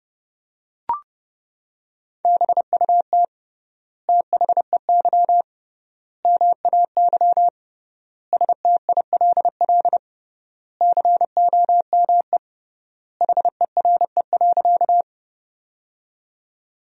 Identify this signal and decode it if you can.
Morse code, heard as a keyed tone, about 32 wpm, 700 Hz. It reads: BUT THEY MAY STILL COME HERE.
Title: 3x Word-Spacing - Sentences from Top 300 Words - 30wpm